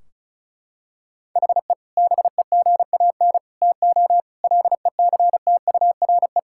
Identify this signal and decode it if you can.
Morse code, heard as a keyed tone, about 36 wpm, 700 Hz. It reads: HE BEGAN TO LECTURE
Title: Sentences from Top 1600 Words 35wpm